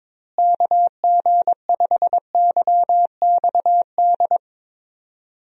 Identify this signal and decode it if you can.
Morse code, heard as a keyed tone, about 22 wpm, 700 Hz. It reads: KG5YXD
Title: USA Call Signs - 6 Characters - Repeated 3x - 22wpm